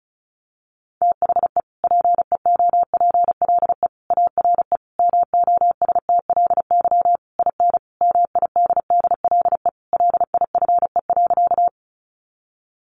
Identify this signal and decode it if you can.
Morse code, heard as a keyed tone, about 34 wpm, 700 Hz. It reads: THE PEOPLE ARE MOSTLY IN MIDDLE LIFE.
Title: Sentences from Top 1300 Words 35wpm